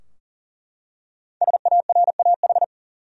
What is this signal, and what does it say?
Morse code, heard as a keyed tone, about 40 wpm, 700 Hz. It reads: SARAH
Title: USA Common Names - Max 5 Letters - 40wpm